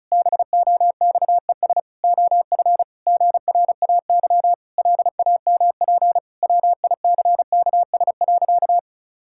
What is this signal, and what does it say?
Morse code, heard as a keyed tone, about 36 wpm, 700 Hz. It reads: BOXES OF GRAY LAMP WICKS.
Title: Pangram Challenge v2 - 35wpm